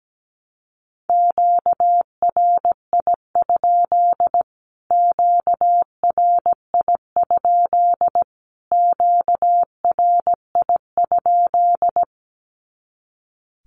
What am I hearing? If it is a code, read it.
Morse code, heard as a keyed tone, about 17 wpm, 700 Hz. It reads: QRI? QRI? QRI?
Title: Q-Codes Repeated 3x 17wpm